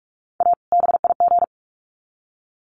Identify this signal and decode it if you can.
Morse code, heard as a keyed tone, about 44 wpm, 700 Hz. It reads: A BIG
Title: Sets of 2 Words 45wpm